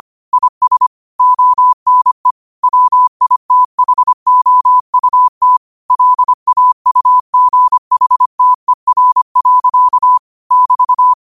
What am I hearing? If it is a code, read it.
Morse code, heard as a keyed tone, about 25 wpm, 1000 Hz. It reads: IS ONE WITHOUT LAUGHTER. <BT>
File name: qotdfiles / 25wpm